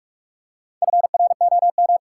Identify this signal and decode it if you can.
Morse code, heard as a keyed tone, about 45 wpm, 700 Hz. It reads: FROM
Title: Four Letter Words - 45wpm